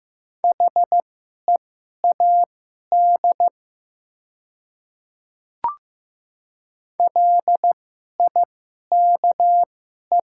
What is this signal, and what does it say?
Morse code, heard as a keyed tone, about 15 wpm, spaced as for 11 wpm, 700 Hz. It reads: HEAD LIKE
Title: 2x Character Spacing - Top 500 Words - 15wpm